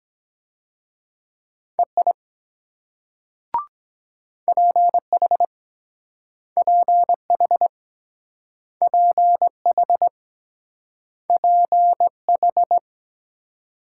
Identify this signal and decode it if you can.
Morse code, heard as a keyed tone, about 24 wpm, 700 Hz. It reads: EI PH PH PH PH